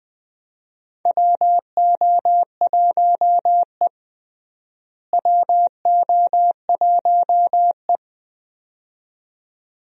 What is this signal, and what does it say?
Morse code, heard as a keyed tone, about 20 wpm, 700 Hz. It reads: WO1E WO1E
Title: USA Call Signs - 4 Characters - Repeated 2x - 20wpm